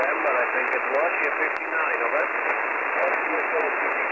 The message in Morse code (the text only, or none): none